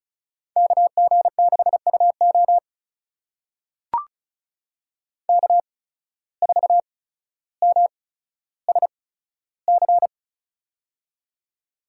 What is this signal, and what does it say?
Morse code, heard as a keyed tone, about 36 wpm, spaced as for 15 wpm, 700 Hz. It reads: KG6UO K4MSC
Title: USA Call Signs - 5 Characters - 8x Character Spacing - 35wpm